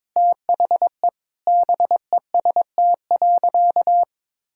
Morse code, heard as a keyed tone, about 22 wpm, 700 Hz. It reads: THE BEST.
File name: practice / 22wpm / Sentences with 4-Word Phrases - 22wpm